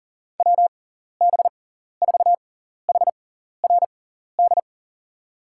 Morse code, heard as a keyed tone, about 40 wpm, spaced as for 14 wpm, 700 Hz. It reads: WB4HRD
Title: USA Call Signs - 6 Characters - 6x Character Spacing - 40wpm